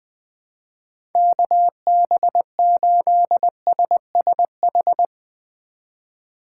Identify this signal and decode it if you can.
Morse code, heard as a keyed tone, about 20 wpm, 700 Hz. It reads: KB8SSH
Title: 2023 POTA Hunter Call Signs - 20wpm